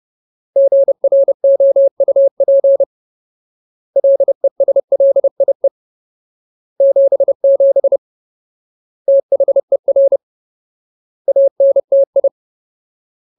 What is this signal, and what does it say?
Morse code, heard as a keyed tone, about 30 wpm, 550 Hz. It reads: GROUP LESLIE 77 THER ANTI
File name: Ultimate Word List / 30WPM Ultimate